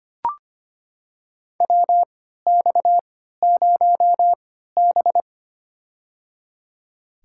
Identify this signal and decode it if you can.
Morse code, heard as a keyed tone, about 25 wpm, spaced as for 14 wpm, 700 Hz. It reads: WX0B
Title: USA Call Signs - 4 Characters - 3x Character Spacing - 25wpm